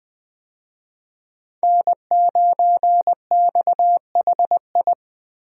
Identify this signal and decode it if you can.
Morse code, heard as a keyed tone, about 20 wpm, 700 Hz. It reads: N9XHI
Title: USA Call Signs - 5 Characters - Repeated 3x - 20wpm